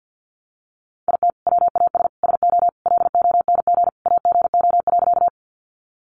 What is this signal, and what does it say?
Morse code, heard as a keyed tone, about 50 wpm, 700 Hz. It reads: IT WAS SO LONG AGO.
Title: Sentences from Top 500 Words 50wpm